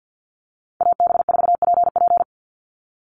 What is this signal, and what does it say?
Morse code, heard as a keyed tone, about 49 wpm, 700 Hz. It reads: AB4PP
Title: NA SOTA Chaser Callsigns 50wpm